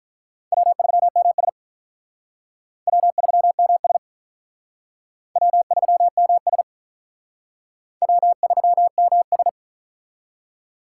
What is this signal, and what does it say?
Morse code, heard as a keyed tone, about 46 wpm, 700 Hz. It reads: W3MS W3MS W3MS W3MS